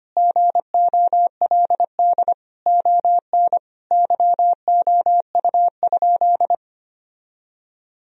Morse code, heard as a keyed tone, about 25 wpm, 700 Hz. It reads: GOLD ON YOU?
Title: Sentences from Top 1000 Words - Review Part 3 - 25wpm